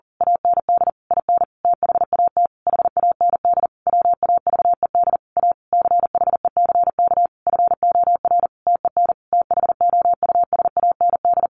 Morse code, heard as a keyed tone, about 37 wpm, 700 Hz. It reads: AND IN THAT HAND WAVED A CHECK FOR TEN THOUSAND
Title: A Sound Of Thunder 40wpm